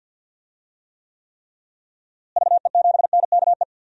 Morse code, heard as a keyed tone, about 52 wpm, 700 Hz. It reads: VE7NKE
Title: DXCC-100 Call Signs - 6 Characters - Repeated 3x - 50wpm